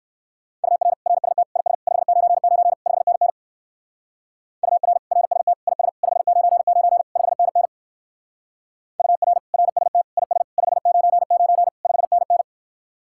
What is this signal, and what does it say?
Morse code, heard as a keyed tone, about 66 wpm, 700 Hz. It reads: UR RST IS 599 5NN UR RST IS 599 5NN UR RST IS 599 5NN